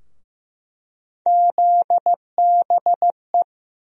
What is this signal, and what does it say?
Morse code, heard as a keyed tone, about 15 wpm, 700 Hz. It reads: ZBE
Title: USA Call Sign Postfixes - 3 Letters 15wpm